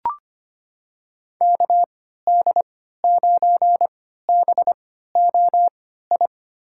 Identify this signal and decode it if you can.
Morse code, heard as a keyed tone, about 25 wpm, spaced as for 14 wpm, 690 Hz. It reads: KD9BOI
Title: USA Call Signs - 6 Characters - 3x Character Spacing - 25wpm